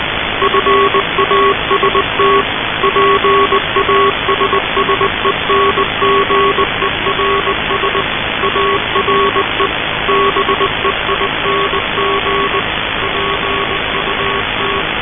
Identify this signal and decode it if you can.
Morse code, heard as a keyed tone, about 20 wpm, 390 Hz, with noise. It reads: FAST PASSENGERS ARE BEING PUT
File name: luxorion / Radio / titanic-21wpm